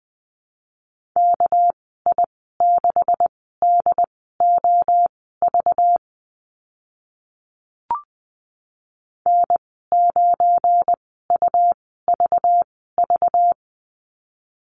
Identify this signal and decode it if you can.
Morse code, heard as a keyed tone, about 20 wpm, spaced as for 15 wpm, 700 Hz. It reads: KI6DOV N9UVV
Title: 213 Lesson 34g - Number 8 - Call Signs 20wpm